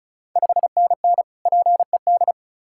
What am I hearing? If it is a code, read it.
Morse code, heard as a keyed tone, about 34 wpm, 690 Hz. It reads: 5NN PED
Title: Ontario QSO Party Exchange 35wpm